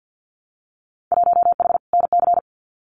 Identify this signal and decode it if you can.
Morse code, heard as a keyed tone, about 50 wpm, 700 Hz. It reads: 1H NC
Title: Winter Field Day Exchanges 50wpm